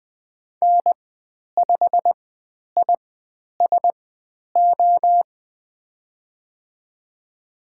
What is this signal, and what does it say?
Morse code, heard as a keyed tone, about 20 wpm, spaced as for 10 wpm, 700 Hz. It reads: N5ISO
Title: 102 Lesson 17g - Letter u - Call Signs 20wpm